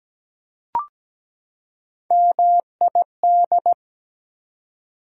Morse code, heard as a keyed tone, about 17 wpm, 700 Hz. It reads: MID
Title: Prefixes and Suffixes 17wpm